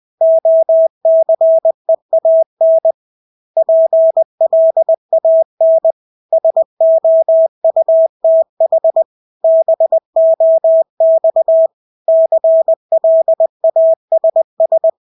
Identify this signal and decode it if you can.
Morse code, heard as a keyed tone, about 20 wpm, 650 Hz. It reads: OCEAN PLAN SOUTH BOX CLASS
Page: Engelske ord 20 wpm